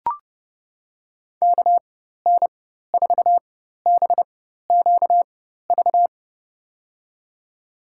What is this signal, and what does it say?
Morse code, heard as a keyed tone, about 30 wpm, spaced as for 14 wpm, 700 Hz. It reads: KN4BQV
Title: USA Call Signs - 6 Characters - 4x Character Spacing - 30wpm